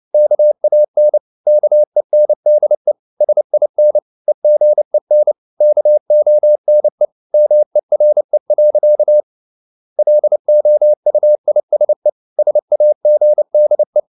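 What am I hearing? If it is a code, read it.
Morse code, heard as a keyed tone, about 29 wpm, 600 Hz. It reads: KAN KENDE SIN EGEN KONE MERE. LOUISE SAGDE
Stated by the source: Fast (QRQ) Morse at 29 wpm, from CW med Gnister.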